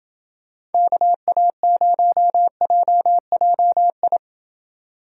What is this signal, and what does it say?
Morse code, heard as a keyed tone, about 28 wpm, 700 Hz. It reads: KA0JJI